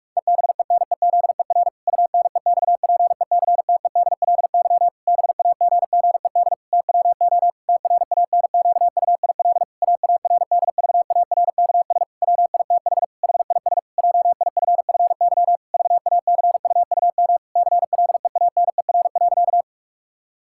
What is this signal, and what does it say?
Morse code, heard as a keyed tone, about 46 wpm, 700 Hz. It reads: EBENEZER UNEXPECTEDLY BAGGED TWO TRANQUIL AARDVARKS WITH HIS JIFFY VACUUM CLEANER.
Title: Pangram Challenge v2 - 45wpm